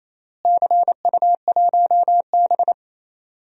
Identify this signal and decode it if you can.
Morse code, heard as a keyed tone, about 28 wpm, 700 Hz. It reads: CU1B
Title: DXCC-100 Call Signs - 4 Characters - Repeated 3x - 28wpm